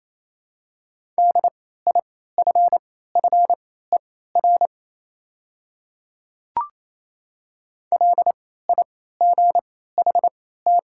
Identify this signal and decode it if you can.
Morse code, heard as a keyed tone, about 28 wpm, spaced as for 16 wpm, 700 Hz. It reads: DIFFER LIGHT
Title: 3x Character Spacing - Top 200 Words - 28wpm